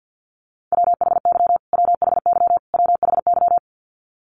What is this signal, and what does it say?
Morse code, heard as a keyed tone, about 50 wpm, 700 Hz. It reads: WHY WHY WHY
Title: Top 200 Words Repeated 50wpm